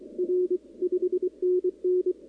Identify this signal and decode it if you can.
Morse code, heard as a keyed tone, about 27 wpm, 340 Hz. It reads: R 5NN
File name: contests / 2019-cw-wpx-kt5j-k5pi- -2330 0-11